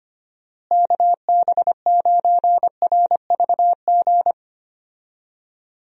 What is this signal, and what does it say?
Morse code, heard as a keyed tone, about 25 wpm, 700 Hz. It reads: KB9RVG
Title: USA Call Signs - 6 Characters - Repeated 2x - 25wpm